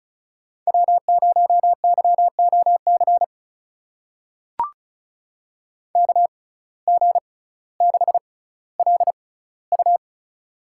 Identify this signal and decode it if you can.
Morse code, heard as a keyed tone, about 36 wpm, spaced as for 18 wpm, 700 Hz. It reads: W0YOC KG6LU
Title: USA Call Signs - 5 Characters - 6x Character Spacing - 35wpm